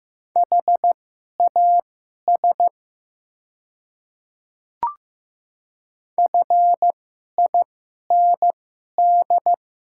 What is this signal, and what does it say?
Morse code, heard as a keyed tone, about 15 wpm, spaced as for 11 wpm, 700 Hz. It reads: HAS FIND
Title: 2x Character Spacing - Top 100 Words - 15wpm